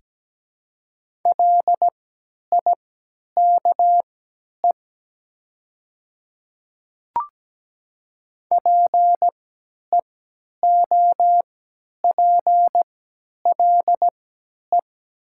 Morse code, heard as a keyed tone, about 17 wpm, spaced as for 10 wpm, 700 Hz. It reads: LIKE PEOPLE
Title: 3x Character Spacing - Top 100 Words - 17wpm